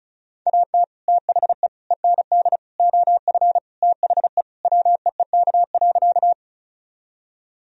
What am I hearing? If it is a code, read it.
Morse code, heard as a keyed tone, about 34 wpm, 700 Hz. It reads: AT THE END OF THE WEEK.
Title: Sentences from Top 1000 Words - Review Part 2 - 35wpm